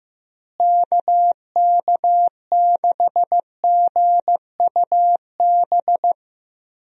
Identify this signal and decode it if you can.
Morse code, heard as a keyed tone, about 15 wpm, 700 Hz. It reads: KK6GUB
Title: USA Call Signs - 6 Characters - Repeated 3x - 15wpm